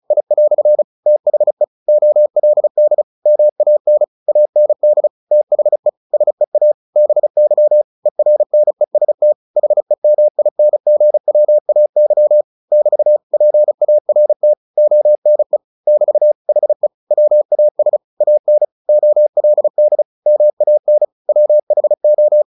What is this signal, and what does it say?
Morse code, heard as a keyed tone, about 35 wpm, 600 Hz. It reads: I" THE OLD MAN AND THE SEA BY ERNEST HEMINGWAY <BT> PART ONE <BT> HE WAS AN OLD MAN WHO
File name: the old man and the sea / the old man and the sea 1 35wpm